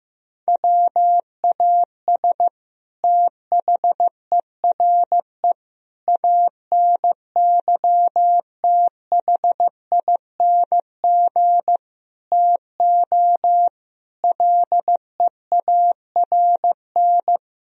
Morse code, heard as a keyed tone, about 15 wpm, 700 Hz. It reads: WAS THERE ANYTHING TO LEARN
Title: Sentences from Top 1000 Words - Review Part 1 - 15wpm